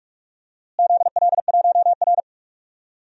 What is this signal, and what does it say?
Morse code, heard as a keyed tone, about 46 wpm, 700 Hz. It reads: ZL1R
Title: DXCC-100 Call Signs - 4 Characters - Repeated 3x - 45wpm